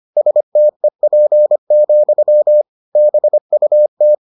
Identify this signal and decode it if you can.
Morse code, heard as a keyed tone, about 25 wpm, 600 Hz. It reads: STEP, BUT